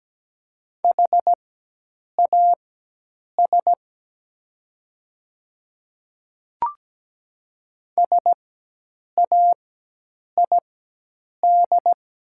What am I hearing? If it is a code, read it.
Morse code, heard as a keyed tone, about 17 wpm, spaced as for 8 wpm, 700 Hz. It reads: HAS SAID
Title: 4x Character Spacing - Top 100 Words - 17wpm